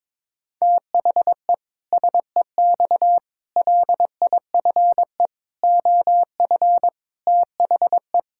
Morse code, heard as a keyed tone, about 22 wpm, 700 Hz. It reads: THE SEX LIFE OF THE
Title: Pangram Challenge v2 - 22wpm